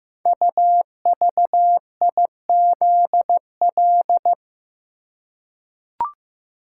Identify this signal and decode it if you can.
Morse code, heard as a keyed tone, about 15 wpm, 700 Hz. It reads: UVIZL
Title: Code Groups - 5 Letters - 15wpm